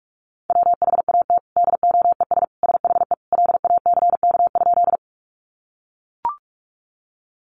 Morse code, heard as a keyed tone, about 44 wpm, 700 Hz. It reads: WHAT DOES SHE LACK?
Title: Sentences from Top 1100 Words 45wpm